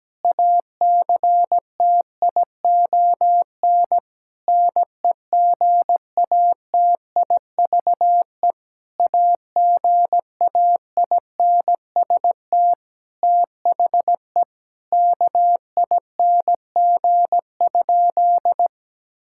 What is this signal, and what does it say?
Morse code, heard as a keyed tone, about 17 wpm, 700 Hz. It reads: ACTION NEGATIVE AGAINST THE KING?
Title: Sentences from Top 1000 Words - Review Part 6 - 17wpm